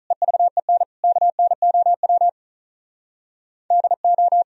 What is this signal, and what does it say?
Morse code, heard as a keyed tone, about 45 wpm, 700 Hz. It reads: EVEN KNOW DO